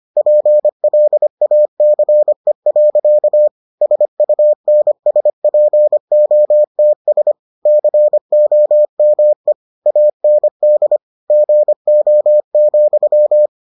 Morse code, heard as a keyed tone, about 25 wpm, 600 Hz. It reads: PLACE. SUNSPOTS COME AND GO,